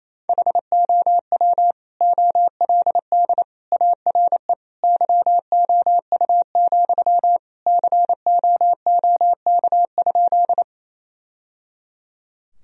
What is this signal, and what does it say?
Morse code, heard as a keyed tone, about 28 wpm, 700 Hz. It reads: HOW OLD ARE YOU, COOK?
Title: Sentences from Top 1000 Words - Review Part 4 - 28wpm